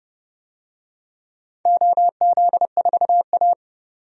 Morse code, heard as a keyed tone, about 30 wpm, 700 Hz. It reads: OZ4A
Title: DXCC-100 Call Signs - 4 Characters - 30wpm